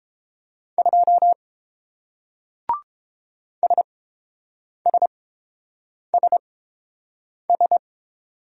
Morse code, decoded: 2 SSSS

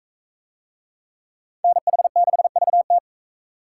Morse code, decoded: NH6VT

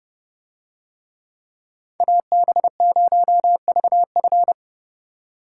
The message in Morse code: AB0VF

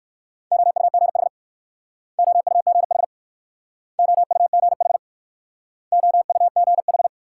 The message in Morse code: OUGH OUGH OUGH OUGH

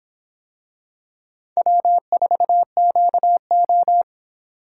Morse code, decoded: W4QO

